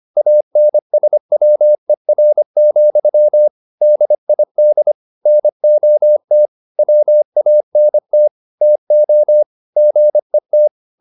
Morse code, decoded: ANSWER, DID NOT WANT TO GET

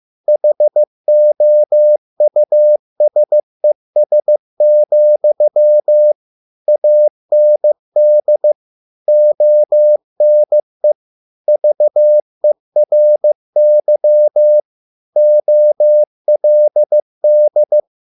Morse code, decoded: HOUSES, AND ONE VERY OLD

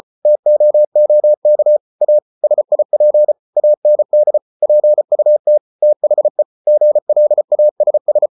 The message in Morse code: TOOK A SIP AND PUT THE GLASS

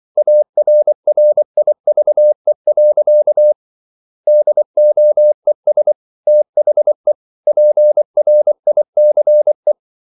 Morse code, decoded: ARRIVE. DOES THE PRICE

